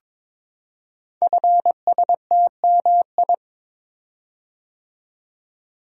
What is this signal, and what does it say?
Morse code, heard as a keyed tone, about 23 wpm, 700 Hz. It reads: FSTMI
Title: Code Groups - 5 Letters - 22wpm